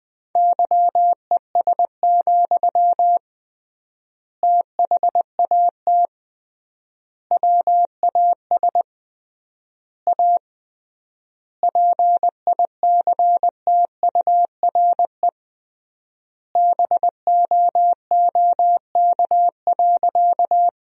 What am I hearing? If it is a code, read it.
Morse code, heard as a keyed tone, about 20 wpm, 700 Hz. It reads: YES, THAT WAS A PICTURE BOOK.
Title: 3x Word-Spacing - Sentences from Top 500 Words - 20wpm